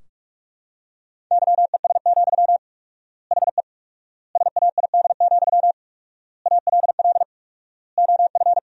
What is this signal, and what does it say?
Morse code, heard as a keyed tone, about 46 wpm, 700 Hz. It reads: YES, HE SAID, ALL OF
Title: 4x Word-Spacing - Sentences from Top 500 Words - 45wpm